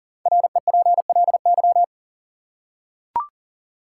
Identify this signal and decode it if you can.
Morse code, heard as a keyed tone, about 40 wpm, 700 Hz. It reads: REPLY
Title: Five Letter Words - 40wpm